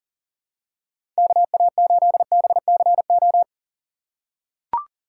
Morse code, heard as a keyed tone, about 40 wpm, 700 Hz. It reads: KA8BCO
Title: USA Call Signs - 2x Character Spacing - 40wpm